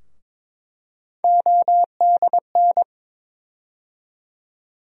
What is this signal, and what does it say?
Morse code, heard as a keyed tone, about 22 wpm, 700 Hz. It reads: ODN
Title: USA Call Sign Postfixes - 3 Letters 22wpm